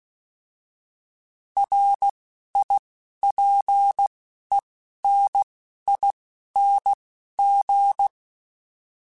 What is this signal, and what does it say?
Morse code, heard as a keyed tone, about 16 wpm, 800 Hz. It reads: R I P E N I N G